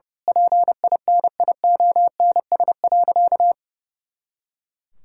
Morse code, heard as a keyed tone, about 30 wpm, 700 Hz. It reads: PINIONS.
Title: Pangram Challenge v2 - 30wpm